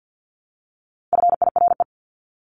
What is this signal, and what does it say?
Morse code, heard as a keyed tone, about 50 wpm, 700 Hz. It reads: FIRE
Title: Top 500 Words Repeated 50wpm